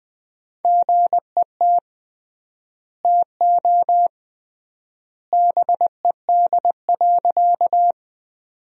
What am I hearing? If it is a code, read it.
Morse code, heard as a keyed tone, about 20 wpm, 700 Hz. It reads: GET TO BED.
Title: 3x Word-Spacing - Sentences from Top 500 Words - 20wpm